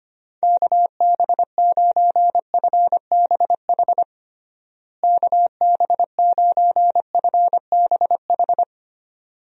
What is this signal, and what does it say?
Morse code, heard as a keyed tone, about 25 wpm, 700 Hz. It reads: KB9FBH KB9FBH
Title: USA Call Signs - 6 Characters - Repeated 2x - 25wpm